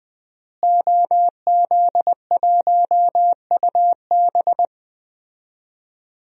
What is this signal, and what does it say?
Morse code, heard as a keyed tone, about 20 wpm, 700 Hz. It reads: OZ1UB
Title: DXCC-100 Call Signs - 5 Characters - Repeated 2x - 20wpm